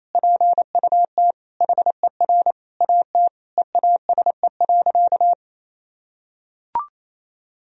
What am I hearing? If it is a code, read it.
Morse code, heard as a keyed tone, about 27 wpm, 700 Hz. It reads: PUT HER AT EASE.
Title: Sentences from Top 1000 Words - Review Part 1 - 28wpm